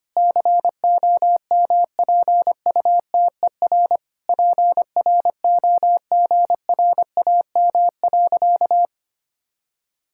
Morse code, heard as a keyed tone, about 25 wpm, 700 Hz. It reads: COMPUTER PROGRAM.
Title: Sentences from Top 1000 Words - Review Part 5 - 25wpm